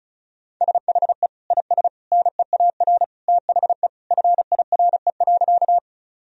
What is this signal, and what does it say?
Morse code, heard as a keyed tone, about 34 wpm, 700 Hz. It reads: SHE IS NEAR THE FIRE.
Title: Sentences from Top 1000 Words - Review Part 1 - 35wpm